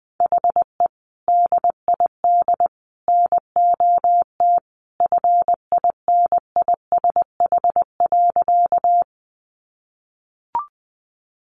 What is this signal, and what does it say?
Morse code, heard as a keyed tone, about 20 wpm, 700 Hz. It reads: HE DID NOT FINISH.
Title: Sentences from Top 1200 Words 20wpm